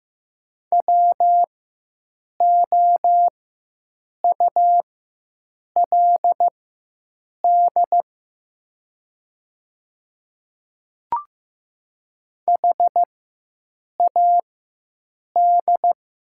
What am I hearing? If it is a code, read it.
Morse code, heard as a keyed tone, about 15 wpm, spaced as for 7 wpm, 700 Hz. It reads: WOULD HAD
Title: 4x Character Spacing - Top 100 Words - 15wpm